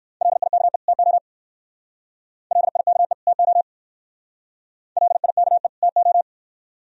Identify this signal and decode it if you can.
Morse code, heard as a keyed tone, about 56 wpm, 700 Hz. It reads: LIKE TO LIKE TO LIKE TO
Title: Speed-Racing - Sets of 2 Words 45wpm